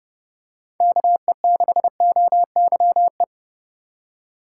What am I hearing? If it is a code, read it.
Morse code, heard as a keyed tone, about 32 wpm, 700 Hz. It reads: KE6OYE
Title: USA Call Signs - 2x Character Spacing - 30wpm